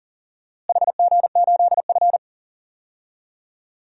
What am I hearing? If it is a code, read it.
Morse code, heard as a keyed tone, about 40 wpm, 700 Hz. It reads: HG8F
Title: DXCC-100 Call Signs - 4 Characters - Repeated 2x - 40wpm